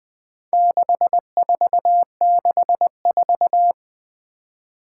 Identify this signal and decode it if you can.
Morse code, heard as a keyed tone, about 20 wpm, 700 Hz. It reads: 6464
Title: Four Challenging Letter-Number Combinations 20wpm